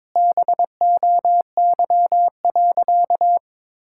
Morse code, heard as a keyed tone, about 22 wpm, 700 Hz. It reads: BOY.
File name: practice / 22wpm / Sentences with 8-Word Phrases - 22wpm